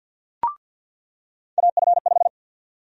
Morse code, heard as a keyed tone, about 46 wpm, 700 Hz. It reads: AF5
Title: USA Call Sign Prefixes - 50wpm